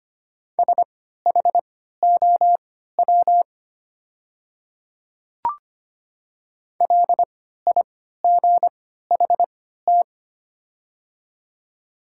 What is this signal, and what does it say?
Morse code, heard as a keyed tone, about 25 wpm, spaced as for 14 wpm, 700 Hz. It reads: SHOW LIGHT
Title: 3x Character Spacing - Top 300 Words - 25wpm